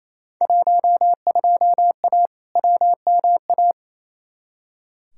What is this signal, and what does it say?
Morse code, heard as a keyed tone, about 28 wpm, 700 Hz. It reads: 12A WMA
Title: Field Day Exchanges Encore - 28wpm